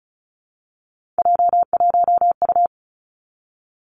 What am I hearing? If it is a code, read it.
Morse code, heard as a keyed tone, about 36 wpm, 700 Hz. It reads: J1U
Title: Three Challenging Letter-Number Combinations 35wpm